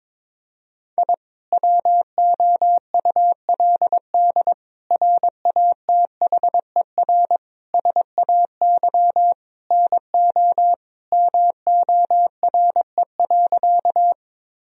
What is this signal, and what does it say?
Morse code, heard as a keyed tone, about 21 wpm, 700 Hz. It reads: I WOULD RATHER SAY NO MORE.
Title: Sentences from Top 1000 Words - Review Part 4 - 22wpm